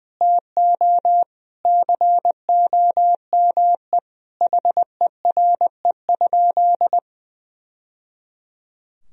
Morse code, heard as a keyed tone, about 21 wpm, 700 Hz. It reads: TO COME HERE?
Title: Sentences from Top 1000 Words - Review Part 4 - 20wpm